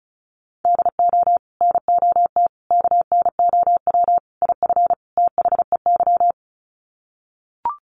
Morse code, heard as a keyed tone, about 34 wpm, 700 Hz. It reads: DO NOT KNOW IF THEY
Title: Sets of 5 Words Encore 35wpm